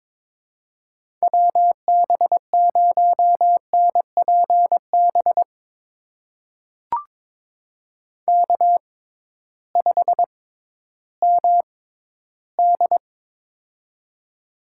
Morse code, heard as a keyed tone, about 23 wpm, spaced as for 13 wpm, 700 Hz. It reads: WB0NPB K5MD
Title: USA Call Signs - 6x Character Spacing - 22wpm